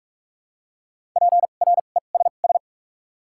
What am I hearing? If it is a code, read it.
Morse code, heard as a keyed tone, about 48 wpm, spaced as for 33 wpm, 700 Hz. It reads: PRESS
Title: 2x Character Spacing - Top 300 Words - 45wpm